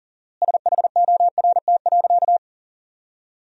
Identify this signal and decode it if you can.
Morse code, heard as a keyed tone, about 40 wpm, 700 Hz. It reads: SHORT.
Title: 4x Word-Spacing - Sentences from Top 500 Words - 40wpm